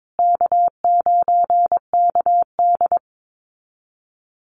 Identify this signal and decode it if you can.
Morse code, heard as a keyed tone, about 22 wpm, 700 Hz. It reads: K9KD
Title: USA Call Signs Repeated 3x 22wpm